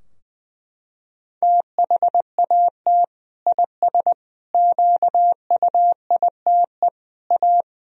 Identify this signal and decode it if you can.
Morse code, heard as a keyed tone, about 21 wpm, 700 Hz. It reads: THAT IS QUITE A
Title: Sentences from Top 1000 Words - Review Part 3 - 20wpm